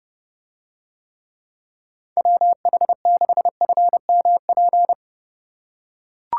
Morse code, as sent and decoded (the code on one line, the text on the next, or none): .-- .... -.... ..-. -- .--.
WH6FMP